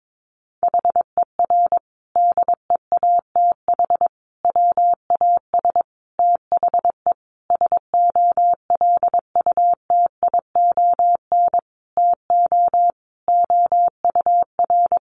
Morse code, heard as a keyed tone, about 22 wpm, 700 Hz. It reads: HER DEATH WAS THE SOLUTION TO OUR
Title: Sentences from Top 1300 Words 22wpm